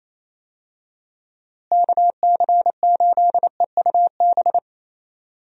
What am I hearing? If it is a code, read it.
Morse code, heard as a keyed tone, about 28 wpm, 700 Hz. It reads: KC8EUB
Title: USA Call Signs - 6 Characters - Repeated 2x - 28wpm